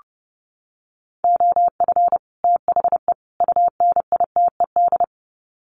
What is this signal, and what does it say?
Morse code, heard as a keyed tone, about 28 wpm, 700 Hz. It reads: OF THE UNITED
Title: Sets of 3 Words 30wpm